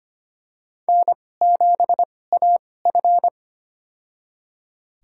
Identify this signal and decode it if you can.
Morse code, heard as a keyed tone, about 25 wpm, spaced as for 18 wpm, 700 Hz. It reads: N7AF
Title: USA Call Signs - 4 Characters - 2x Character Spacing - 25wpm